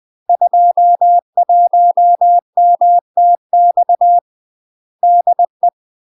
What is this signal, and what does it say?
Morse code, heard as a keyed tone, about 20 wpm, 700 Hz. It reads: 21MTX DE
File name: UK MASTER QSOs TRAINING / 20WPM / UK MASTER QSOs 0053